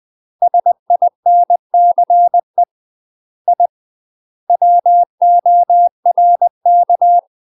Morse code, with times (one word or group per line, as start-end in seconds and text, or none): 0.4-2.6 SINCE
3.5-3.7 I
4.5-7.2 WORK